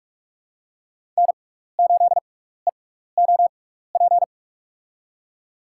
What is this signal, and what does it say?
Morse code, heard as a keyed tone, about 46 wpm, spaced as for 16 wpm, 700 Hz. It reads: N8EOP
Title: USA Call Signs - 5 Characters - 6x Character Spacing - 45wpm